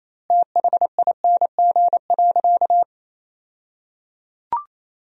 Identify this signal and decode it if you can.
Morse code, heard as a keyed tone, about 28 wpm, 700 Hz. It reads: THING.
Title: 2x Word-Spacing - Sentences from Top 400 Words - 28wpm